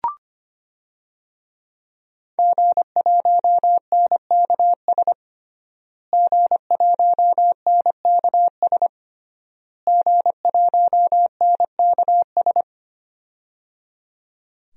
Morse code, heard as a keyed tone, about 25 wpm, 700 Hz. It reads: G1NKS G1NKS G1NKS
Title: DXCC-100 Call Signs - 5 Characters - Repeated 3x - 25wpm